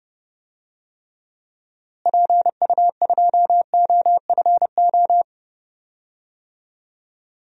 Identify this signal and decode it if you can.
Morse code, heard as a keyed tone, about 30 wpm, 700 Hz. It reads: PU2OFO
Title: DXCC-100 Call Signs - 6 Characters - Repeated 3x - 30wpm